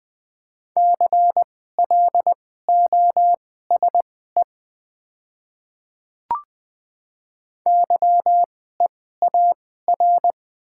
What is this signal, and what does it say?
Morse code, heard as a keyed tone, about 20 wpm, spaced as for 14 wpm, 700 Hz. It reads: CLOSE YEAR
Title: 2x Character Spacing - Top 300 Words - 20wpm